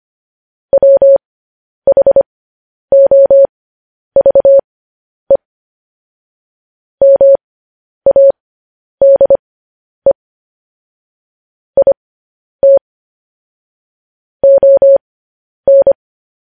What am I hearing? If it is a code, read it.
Morse code, heard as a keyed tone, about 25 wpm, spaced as for 10 wpm, 550 Hz. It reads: WHOVE MADE IT ON
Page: News Headlines in Morse Code at 10 WPM